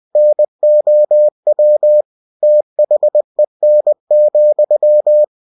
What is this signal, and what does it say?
Morse code, heard as a keyed tone, about 20 wpm, 600 Hz. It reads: NOW THEN,
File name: cw-books / winnie-the-pooh / winnie-20wpm-page-0099